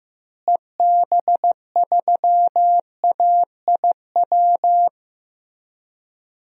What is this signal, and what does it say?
Morse code, heard as a keyed tone, about 15 wpm, 700 Hz. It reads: EB3AIW
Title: DXCC-100 Call Signs - 6 Characters - Repeated 3x - 15wpm